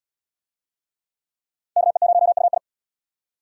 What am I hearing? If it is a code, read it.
Morse code, heard as a keyed tone, about 76 wpm, 700 Hz. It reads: KE0UI